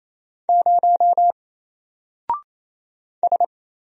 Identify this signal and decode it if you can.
Morse code, heard as a keyed tone, about 27 wpm, 700 Hz. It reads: 0 S
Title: Single Character (Letter:Number:Punctuation:Prosign) 28wpm